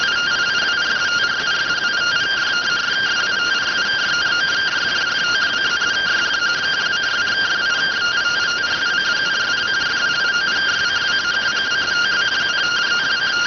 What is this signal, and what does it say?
Two-tone teleprinter signal, about 170 Hz shift.